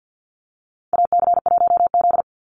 Morse code, heard as a keyed tone, about 48 wpm, 700 Hz. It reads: AC1Z
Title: NA SOTA Activator Callsigns 50wpm